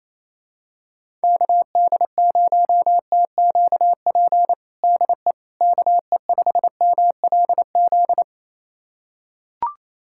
Morse code, heard as a keyed tone, about 28 wpm, 700 Hz. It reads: KD0TQP DE KE5MLZ
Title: USA Callsign DE Callsign - 6 Characters - 28wpm